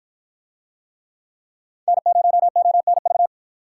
Morse code, heard as a keyed tone, about 53 wpm, 700 Hz. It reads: N0ONV